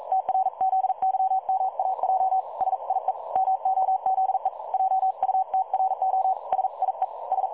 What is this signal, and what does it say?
Morse code, heard as a keyed tone, about 42 wpm, 760 Hz. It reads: TRZYMAJ SIE MOZE JUTRO SIE S
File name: dlazaawansowanych / nagraniazpasma / websdr recording 3565.0kHz